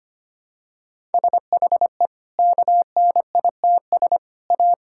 Morse code, heard as a keyed tone, about 25 wpm, 700 Hz. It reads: SHE KNITS A
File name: Easy Sentences with Hints - 25wpm